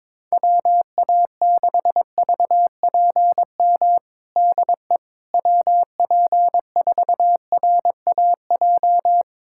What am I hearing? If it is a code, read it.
Morse code, heard as a keyed tone, about 22 wpm, 700 Hz. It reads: WA6VPM DE WP4RAJ
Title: USA Callsign DE Callsign - 6 Characters - 22wpm